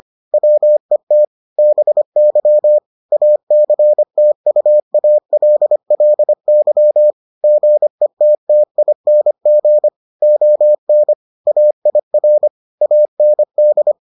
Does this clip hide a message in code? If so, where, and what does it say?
Yes: Morse code from 0.3 to 13.9 s, heard as a keyed tone, about 25 wpm, 600 Hz. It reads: WET BY ACTUALLY GETTING ON AIR AND